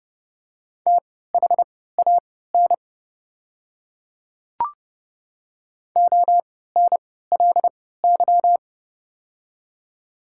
Morse code, heard as a keyed tone, about 29 wpm, spaced as for 17 wpm, 690 Hz. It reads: THAN ONLY
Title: 3x Character Spacing - Top 200 Words - 30wpm